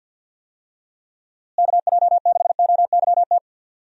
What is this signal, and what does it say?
Morse code, heard as a keyed tone, about 48 wpm, 700 Hz. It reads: KJ6OYT
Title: USA Call Signs - 6 Characters - Repeated 2x - 50wpm